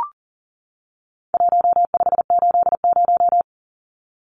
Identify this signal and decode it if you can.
Morse code, heard as a keyed tone, about 40 wpm, 700 Hz. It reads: 1580
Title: Numbers - Contest Style - 4-Digit 40wpm